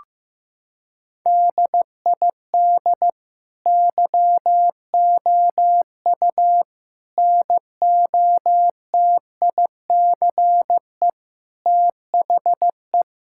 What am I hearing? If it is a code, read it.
Morse code, heard as a keyed tone, about 15 wpm, 700 Hz. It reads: DID YOU NOTICE THE
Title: Sentences from Top 1000 Words - Review Part 5 - 15wpm